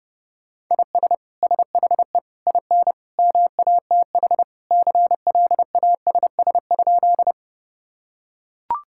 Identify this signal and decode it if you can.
Morse code, heard as a keyed tone, about 32 wpm, 700 Hz. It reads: IS SHE IN MATH CLASS?
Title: Sentences from Top 1000 Words - Review Part 6 - 30wpm